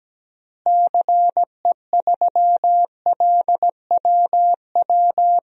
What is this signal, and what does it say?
Morse code, heard as a keyed tone, about 17 wpm, 700 Hz. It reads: CE3LWW
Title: DXCC-100 Call Signs - 6 Characters - Repeated 3x - 17wpm